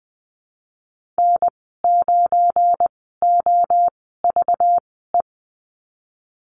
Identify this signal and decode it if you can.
Morse code, heard as a keyed tone, about 20 wpm, spaced as for 15 wpm, 700 Hz. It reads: N9OVE
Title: 193 Lesson 31g - Letter v - Call Signs 20wpm